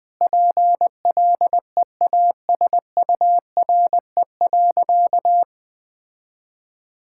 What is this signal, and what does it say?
Morse code, heard as a keyed tone, about 21 wpm, 700 Hz. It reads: PLEASURE.
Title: Sentences from Top 1600 Words 20wpm